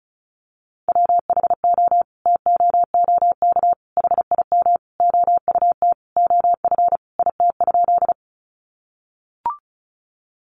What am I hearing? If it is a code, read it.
Morse code, heard as a keyed tone, about 34 wpm, 700 Hz. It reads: WHO TOOK HIM OUT OF IT?
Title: Sentences from Top 300 Words 35wpm